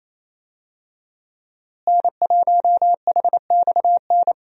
Morse code, heard as a keyed tone, about 27 wpm, 700 Hz. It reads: N1HXN
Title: USA Call Signs - 5 Characters - Repeated 3x - 28wpm